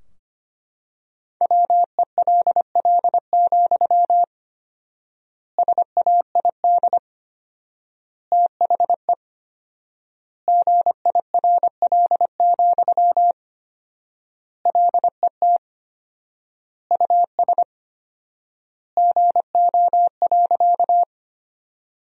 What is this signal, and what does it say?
Morse code, heard as a keyed tone, about 25 wpm, 700 Hz. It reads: WELL, SAID THE GIRL, LET US GO.
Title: 4x Word-Spacing - Sentences from Top 400 Words - 25wpm